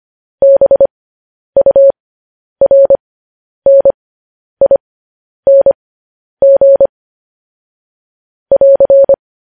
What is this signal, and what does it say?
Morse code, heard as a keyed tone, about 25 wpm, spaced as for 10 wpm, 550 Hz. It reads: BURNING <AR>
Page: News Headlines in Morse Code at 10 WPM